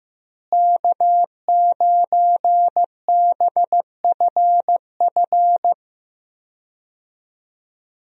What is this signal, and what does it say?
Morse code, heard as a keyed tone, about 15 wpm, 700 Hz. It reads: K9BFF
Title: USA Call Signs - 5 Characters - Repeated 2x - 15wpm